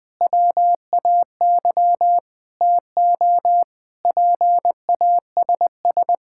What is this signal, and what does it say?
Morse code, heard as a keyed tone, about 21 wpm, 700 Hz. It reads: WAY TO PASS
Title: Difficult Sentences with Hints - 20wpm